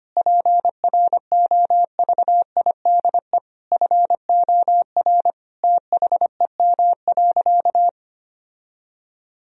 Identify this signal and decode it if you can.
Morse code, heard as a keyed tone, about 25 wpm, 700 Hz. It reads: PROVIDE FOR THEM.
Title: Sentences from Top 1000 Words - Review Part 3 - 25wpm